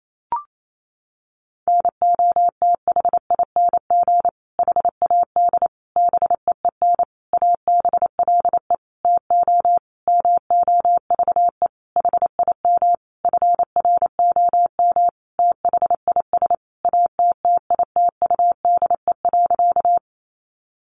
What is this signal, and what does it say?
Morse code, heard as a keyed tone, about 28 wpm, 700 Hz. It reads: NOTHING HAD BEEN ABLE TO MOVE HIM FROM THIS ATTITUDE.
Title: Sentences from Top 1300 Words 28wpm